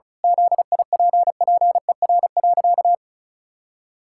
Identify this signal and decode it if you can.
Morse code, heard as a keyed tone, about 36 wpm, 700 Hz. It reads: ZIPPER.
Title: Pangram Challenge v2 - 35wpm